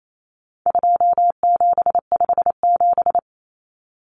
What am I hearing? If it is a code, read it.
Morse code, heard as a keyed tone, about 28 wpm, 700 Hz. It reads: 2757
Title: Common Cut Numbers - 4-Digit 28wpm